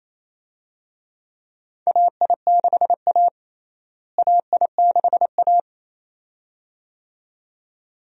Morse code, heard as a keyed tone, about 28 wpm, 700 Hz. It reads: AI6A AI6A
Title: USA Call Signs - 4 Characters - Repeated 2x - 28wpm